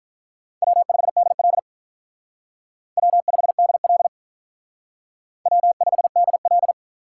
Morse code, heard as a keyed tone, about 45 wpm, 700 Hz. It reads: W5DL W5DL W5DL